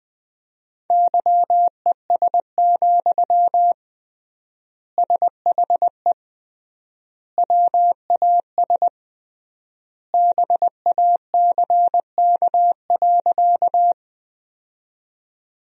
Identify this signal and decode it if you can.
Morse code, heard as a keyed tone, about 20 wpm, 700 Hz. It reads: YES, SHE WAS BACK.
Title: 3x Word-Spacing - Sentences from Top 500 Words - 20wpm